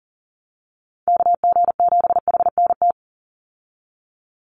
Morse code, heard as a keyed tone, about 40 wpm, 700 Hz. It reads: KG7HNT
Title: USA Call Signs 40wpm